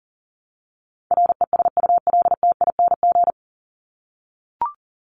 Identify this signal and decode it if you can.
Morse code, heard as a keyed tone, about 40 wpm, 700 Hz. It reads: RESULTING
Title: ING Suffix - Top 100 Words with Base Words 40wpm